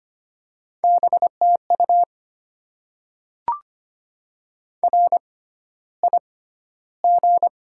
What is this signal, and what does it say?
Morse code, heard as a keyed tone, about 25 wpm, spaced as for 13 wpm, 700 Hz. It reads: BTU RIG
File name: Top 20 QSO Elements - 6x Character Spacing - 25wpm